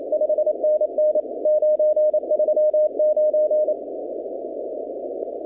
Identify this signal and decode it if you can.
Morse code, heard as a keyed tone, about 29 wpm, 610 Hz. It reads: 5NN 939
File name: contests / 2019-cw-wpx-kt5j-k5pi- -2330 0-20